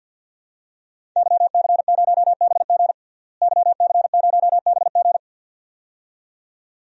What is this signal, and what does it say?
Morse code, heard as a keyed tone, about 49 wpm, 700 Hz. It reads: YC0BG YC0BG
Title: DXCC-100 Call Signs - 5 Characters - Repeated 2x - 50wpm